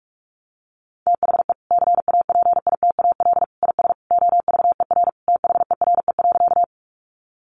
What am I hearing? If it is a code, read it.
Morse code, heard as a keyed tone, about 44 wpm, 700 Hz. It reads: THE CAPITAL IS OVER THERE.
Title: Sentences from Top 1000 Words 45wpm